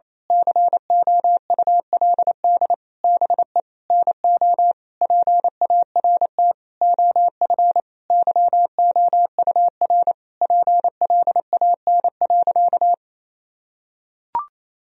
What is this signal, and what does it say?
Morse code, heard as a keyed tone, about 28 wpm, 700 Hz. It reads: COULD BE NO PART OF YOUR PLAN.
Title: Sentences from Top 1000 Words - Review Part 2 - 28wpm